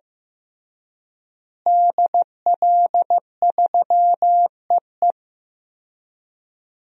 Morse code, heard as a keyed tone, about 15 wpm, 700 Hz. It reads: DL3EE
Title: DXCC-100 Call Signs - 5 Characters - Repeated 3x - 15wpm